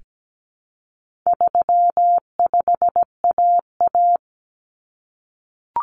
Morse code, heard as a keyed tone, about 17 wpm, 700 Hz. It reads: 35AA
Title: Common Cut Numbers - 4-Digit 17wpm